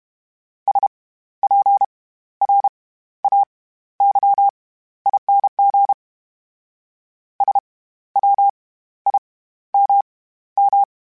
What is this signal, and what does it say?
Morse code, heard as a keyed tone, about 32 wpm, spaced as for 14 wpm, 800 Hz. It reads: SPRAYING SWIMM
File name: Special INGs / Special INGs-mp3s / SlowToFaster ing @30